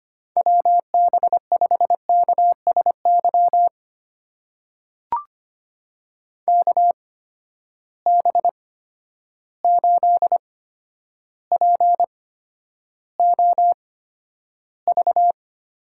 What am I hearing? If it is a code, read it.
Morse code, heard as a keyed tone, about 25 wpm, spaced as for 11 wpm, 700 Hz. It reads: WB5KSY KB8POV
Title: USA Call Signs - 6 Characters - 8x Character Spacing - 25wpm